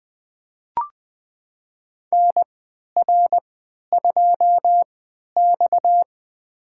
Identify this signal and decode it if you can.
Morse code, heard as a keyed tone, about 20 wpm, spaced as for 11 wpm, 700 Hz. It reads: NR2X
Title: USA Call Signs - 4 Characters - 3x Character Spacing - 20wpm